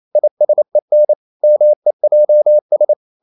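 Morse code, heard as a keyed tone, about 28 wpm, 600 Hz. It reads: ISEN MEJS
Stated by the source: Danish short words sent in Morse at 28 wpm, from CW med Gnister.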